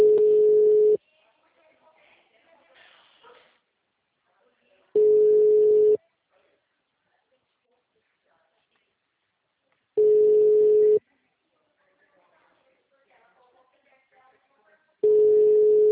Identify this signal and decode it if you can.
Ringback tone (Europe, 425 Hz), with speech.
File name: out- -2007